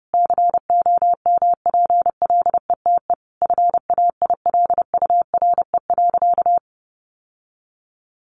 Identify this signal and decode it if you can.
Morse code, heard as a keyed tone, about 30 wpm, 700 Hz. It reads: COMPLETE FAILURE.
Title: Sentences from Top 1400 Words 30wpm